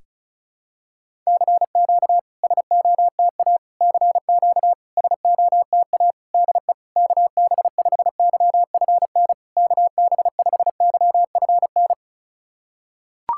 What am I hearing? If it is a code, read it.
Morse code, heard as a keyed tone, about 36 wpm, 690 Hz. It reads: CQ SOTA CQ SOTA DE KB5YFN KB5YFN
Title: SOTA QSO Practice 35wpm